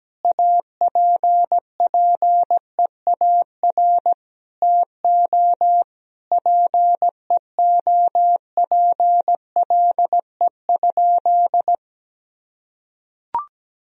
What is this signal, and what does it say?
Morse code, heard as a keyed tone, about 17 wpm, 700 Hz. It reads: APPEAR TO PEOPLE?
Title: Sentences from Top 1000 Words - Review Part 2 - 17wpm